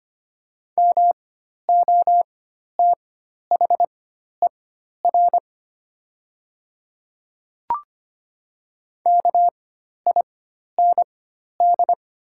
Morse code, heard as a keyed tone, about 25 wpm, spaced as for 12 wpm, 700 Hz. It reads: MOTHER KIND
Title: 4x Character Spacing - Top 200 Words - 25wpm